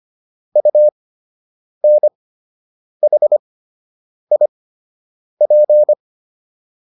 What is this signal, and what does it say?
Morse code, heard as a keyed tone, about 25 wpm, spaced as for 8 wpm, 600 Hz. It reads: UNHIP